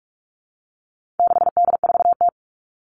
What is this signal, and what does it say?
Morse code, heard as a keyed tone, about 44 wpm, 700 Hz. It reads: 6D4T
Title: All Cut Numbers - 4-Digit 45wpm